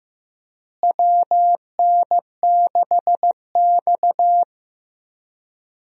Morse code, heard as a keyed tone, about 15 wpm, 700 Hz. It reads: WN6X
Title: USA Call Signs - 4 Characters - Repeated 3x - 15wpm